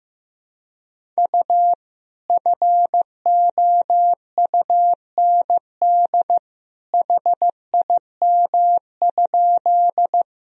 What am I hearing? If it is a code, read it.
Morse code, heard as a keyed tone, about 16 wpm, 700 Hz. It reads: U FOUND HIM?
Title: Sentences from Top 300 Words and Abbreviations 15wpm